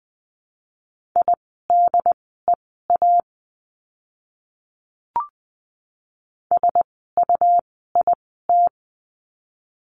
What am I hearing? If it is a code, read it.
Morse code, heard as a keyed tone, about 20 wpm, spaced as for 14 wpm, 700 Hz. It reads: IDEA SUIT
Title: 101 Lesson 17f - Letter u - Words 20wpm